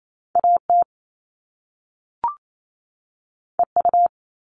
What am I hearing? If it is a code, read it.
Morse code, heard as a keyed tone, about 28 wpm, 700 Hz. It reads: AT EU
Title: All Cut Numbers - 2-Digit 28wpm